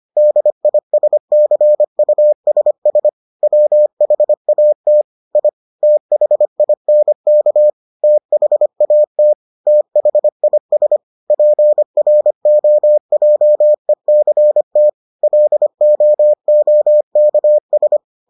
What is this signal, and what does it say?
Morse code, heard as a keyed tone, about 25 wpm, 600 Hz. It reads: DISCUSS WHAT I THINK THAT THIS PROJECT LOOKS